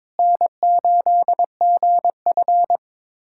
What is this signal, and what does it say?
Morse code, heard as a keyed tone, about 22 wpm, 700 Hz. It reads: N8GF
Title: USA Callsign DE Callsign 22wpm